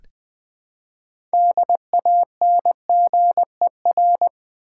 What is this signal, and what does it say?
Morse code, heard as a keyed tone, about 20 wpm, 700 Hz. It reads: DANGER
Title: Build A Word - 3 Letters At A Time - 9-Letter Words - 20wpm